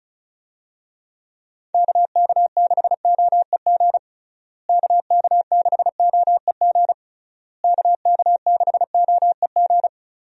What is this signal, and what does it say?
Morse code, heard as a keyed tone, about 34 wpm, 700 Hz. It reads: KK6OEG KK6OEG KK6OEG
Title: USA Call Signs - 6 Characters - Repeated 3x - 35wpm